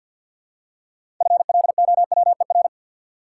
Morse code, heard as a keyed tone, about 50 wpm, 700 Hz. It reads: FLOWER
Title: Six Letter Words - 50wpm